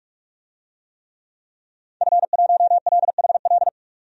Morse code, heard as a keyed tone, about 46 wpm, 700 Hz. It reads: F1LHL
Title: DXCC-100 Call Signs - 5 Characters - 45wpm